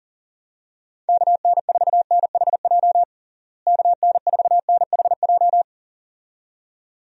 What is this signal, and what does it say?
Morse code, heard as a keyed tone, about 37 wpm, 700 Hz. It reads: KN4NHJ KN4NHJ
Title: USA Call Signs - 6 Characters - Repeated 2x - 40wpm